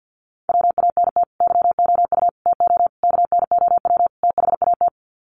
Morse code, heard as a keyed tone, about 50 wpm, 700 Hz. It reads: WANT YOU TO KNOW THAT
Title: Sets of 5 Words Encore 50wpm